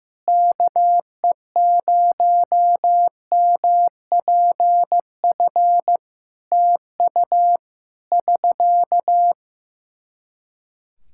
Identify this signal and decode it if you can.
Morse code, heard as a keyed tone, about 15 wpm, 700 Hz. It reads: KE0MPF TU <SK>
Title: Practice QSO - Exchange 4 - Ending - 15wpm